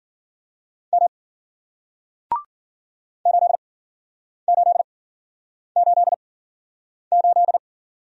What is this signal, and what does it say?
Morse code, heard as a keyed tone, about 55 wpm, spaced as for 9 wpm, 700 Hz. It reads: M 8888